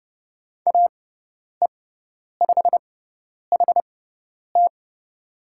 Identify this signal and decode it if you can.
Morse code, heard as a keyed tone, about 30 wpm, spaced as for 10 wpm, 700 Hz. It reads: AE5HT
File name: Lesson 16g - Letter c - Call Signs 30wpm